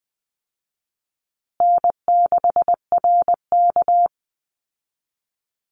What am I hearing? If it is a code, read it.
Morse code, heard as a keyed tone, about 20 wpm, 700 Hz. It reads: N6RK
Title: Easy Call Signs Repeated 2x - 20wpm